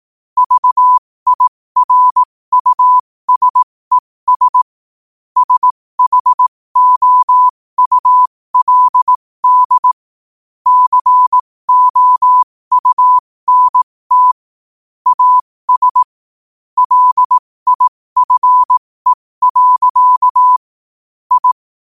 Morse code, heard as a keyed tone, about 18 wpm, spaced as for 15 wpm, 1000 Hz. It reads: VIRUSES SHOULD COUNT AS LIFE. I
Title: 15 WPM morse code quotes for Fri, 01 Mar 2024